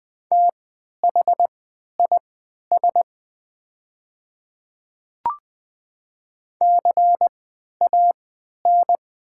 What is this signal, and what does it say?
Morse code, heard as a keyed tone, about 20 wpm, spaced as for 11 wpm, 700 Hz. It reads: THIS CAN
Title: 3x Character Spacing - Top 200 Words - 20wpm